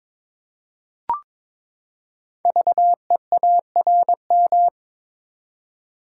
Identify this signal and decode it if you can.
Morse code, heard as a keyed tone, about 22 wpm, 700 Hz. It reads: VEARM
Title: Code Groups - 5 Letters - 22wpm